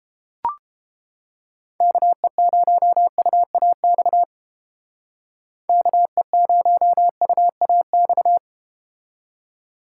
Morse code, heard as a keyed tone, about 31 wpm, 700 Hz. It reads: KE0UAX KE0UAX